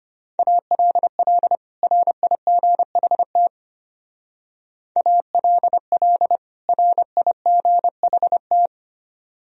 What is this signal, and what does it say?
Morse code, heard as a keyed tone, about 32 wpm, 700 Hz. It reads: ALL RIGHT ALL RIGHT